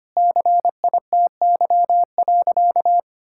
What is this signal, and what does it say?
Morse code, heard as a keyed tone, about 25 wpm, 700 Hz. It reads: CITY.
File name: practice / 25wpm / Sentences with 4-Word Phrases - 25wpm